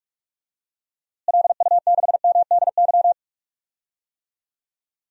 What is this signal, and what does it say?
Morse code, heard as a keyed tone, about 47 wpm, 700 Hz. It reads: LU6MDY
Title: DXCC-100 Call Signs - 6 Characters - Repeated 3x - 45wpm